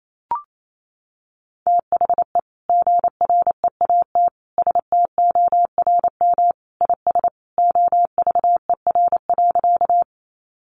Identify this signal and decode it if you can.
Morse code, heard as a keyed tone, about 28 wpm, 700 Hz. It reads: THE GREAT STORM IS OVER.
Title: Sentences from Top 1500 Words 28wpm